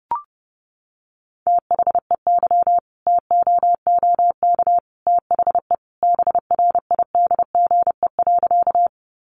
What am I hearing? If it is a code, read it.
Morse code, heard as a keyed tone, about 30 wpm, 700 Hz. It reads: THEY TOOK THE BRIDGE.
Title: Sentences from Top 1200 Words 30wpm